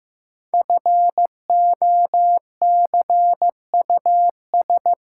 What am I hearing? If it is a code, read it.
Morse code, heard as a keyed tone, about 15 wpm, 700 Hz. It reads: FOCUS
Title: Five Letter Words - 15wpm